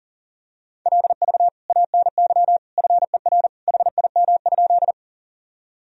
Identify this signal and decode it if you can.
Morse code, heard as a keyed tone, about 40 wpm, 700 Hz. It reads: LV ANY FER HIM?
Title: Sentences from Top 400 Words and Abbreviations 40wpm